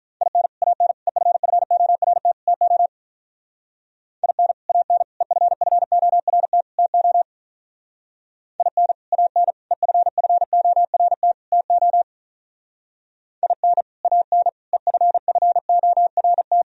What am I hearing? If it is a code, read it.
Morse code, heard as a keyed tone, about 43 wpm, 700 Hz. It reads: IN AN EFFORT TO IN AN EFFORT TO IN AN EFFORT TO IN AN EFFORT
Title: Speed-Racing - Sets of 4 Words 35wpm